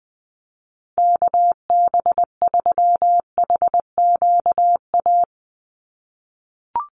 Morse code, heard as a keyed tone, about 20 wpm, 700 Hz. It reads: KB3HQA
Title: USA Call Signs 20wpm